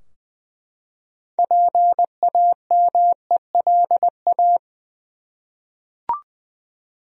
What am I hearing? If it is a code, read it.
Morse code, heard as a keyed tone, about 20 wpm, 700 Hz. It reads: PAMELA
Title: USA Common Names - 20wpm